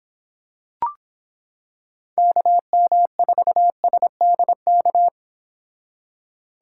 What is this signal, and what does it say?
Morse code, heard as a keyed tone, about 26 wpm, 700 Hz. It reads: KM4SDK